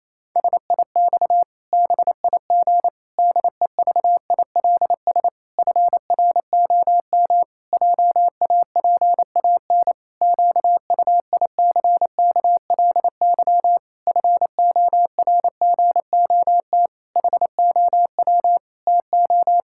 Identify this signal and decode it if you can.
Morse code, heard as a keyed tone, about 28 wpm, 700 Hz. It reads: SIX BIG DEVILS FROM JAPAN QUICKLY FORGOT HOW TO
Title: Pangram Challenge v2 - 28wpm